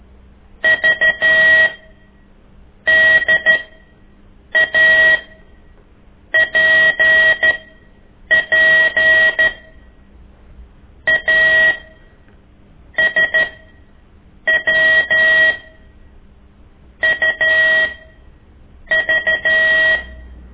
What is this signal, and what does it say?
Morse code, heard as a keyed tone, about 13 wpm, spaced as for 6 wpm, 1900 Hz. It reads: VDAPP ASWUV
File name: ik1weg2 / CORSI CW